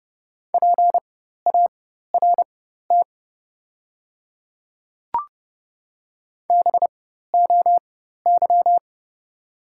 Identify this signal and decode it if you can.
Morse code, heard as a keyed tone, about 31 wpm, spaced as for 14 wpm, 690 Hz. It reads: PART BOY
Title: 4x Character Spacing - Top 500 Words - 30wpm